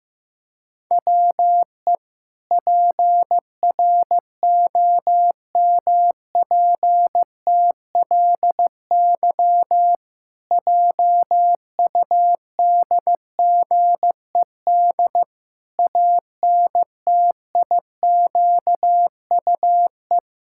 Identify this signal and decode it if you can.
Morse code, heard as a keyed tone, about 15 wpm, 700 Hz. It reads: WE PROMPTLY JUDGED ANTIQUE
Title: Pangram Challenge v2 - 15wpm